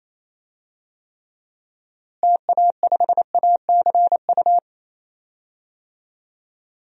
Morse code, heard as a keyed tone, about 28 wpm, 700 Hz. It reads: TA5ACU
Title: DXCC-100 Call Signs - 6 Characters - Repeated 2x - 28wpm